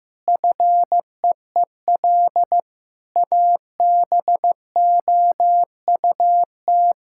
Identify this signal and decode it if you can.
Morse code, heard as a keyed tone, about 15 wpm, 700 Hz. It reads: FEEL ABOUT
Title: Sentences from Top 1000 Words - Review Part 6 - 15wpm